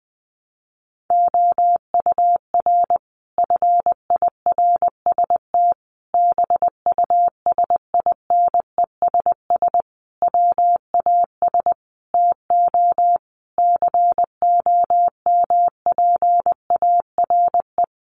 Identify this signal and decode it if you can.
Morse code, heard as a keyed tone, about 20 wpm, 700 Hz. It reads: OUR FIRST BUSINESS WAS TO COMPARE
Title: Sentences from Top 1300 Words 20wpm